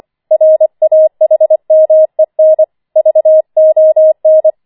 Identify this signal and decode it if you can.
Morse code, heard as a keyed tone, about 25 wpm, 630 Hz. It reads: RAHMEN VON